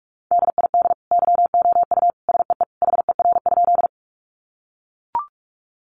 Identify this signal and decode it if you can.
Morse code, heard as a keyed tone, about 44 wpm, 700 Hz. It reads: DID YOU SEE HER?
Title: Sentences from Top 100 Words 45wpm